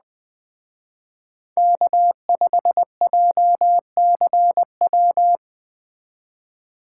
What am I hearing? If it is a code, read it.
Morse code, heard as a keyed tone, about 20 wpm, 700 Hz. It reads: K5JCW
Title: Speed Racing - USA 1x3 Call Signs 17wpm